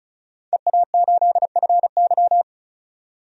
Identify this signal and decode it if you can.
Morse code, heard as a keyed tone, about 36 wpm, 700 Hz. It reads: EA8FY
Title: DXCC-100 Call Signs - 5 Characters - 35wpm